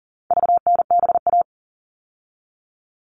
Morse code, heard as a keyed tone, about 40 wpm, 700 Hz. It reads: VNBA
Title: All Cut Numbers - 4-Digit 40wpm